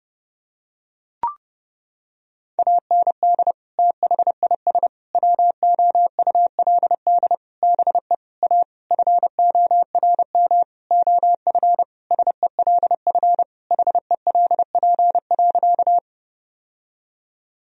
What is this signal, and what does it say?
Morse code, heard as a keyed tone, about 30 wpm, 700 Hz. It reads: AND THIS WOULD BE A FORM OF SELF HELP.
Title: Sentences from Top 1000 Words - Review Part 1 - 30wpm